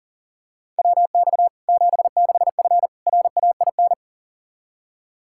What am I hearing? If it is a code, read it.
Morse code, heard as a keyed tone, about 40 wpm, 700 Hz. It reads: WX 76F RAIN
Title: Practice QSO - Exchange 2 - Info - 40wpm